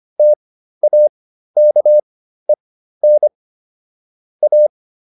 Morse code, heard as a keyed tone, about 25 wpm, spaced as for 13 wpm, 600 Hz. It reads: TAKEN A